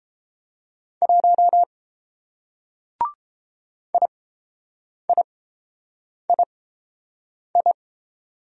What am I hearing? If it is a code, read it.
Morse code, heard as a keyed tone, about 30 wpm, spaced as for 7 wpm, 700 Hz. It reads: 1 IIII